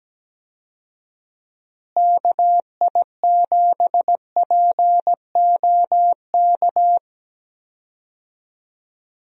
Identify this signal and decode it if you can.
Morse code, heard as a keyed tone, about 17 wpm, 700 Hz. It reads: KI7POK
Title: USA Call Signs - 6 Characters - Repeated 3x - 17wpm